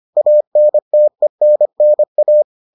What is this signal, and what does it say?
Morse code, heard as a keyed tone, about 25 wpm, 600 Hz. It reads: ANTENNA